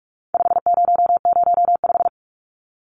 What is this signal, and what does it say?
Morse code, heard as a keyed tone, about 44 wpm, 700 Hz. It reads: 5005
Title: Numbers - 4-Digit 45wpm